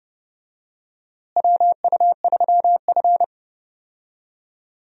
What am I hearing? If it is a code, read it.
Morse code, heard as a keyed tone, about 30 wpm, 700 Hz. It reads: WU3F
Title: USA Call Signs - 4 Characters - Repeated 3x - 30wpm